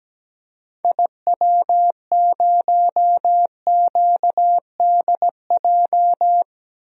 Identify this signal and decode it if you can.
Morse code, heard as a keyed tone, about 17 wpm, 700 Hz. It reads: IW0QDJ
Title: DXCC-100 Call Signs - 6 Characters - 17wpm